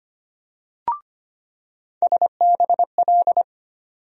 Morse code, heard as a keyed tone, about 25 wpm, 700 Hz. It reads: SBL
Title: USA Call Sign Postfixes - 3 Letters 25wpm